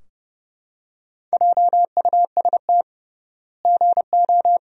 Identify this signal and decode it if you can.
Morse code, heard as a keyed tone, about 30 wpm, 700 Hz. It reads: JUST GO
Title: 3x Word-Spacing - Sentences from Top 500 Words - 30wpm